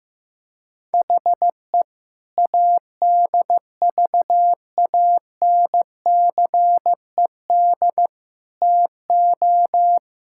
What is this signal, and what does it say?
Morse code, heard as a keyed tone, about 15 wpm, 700 Hz. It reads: HE ADVANCED TO
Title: Sentences from Top 1600 Words 15wpm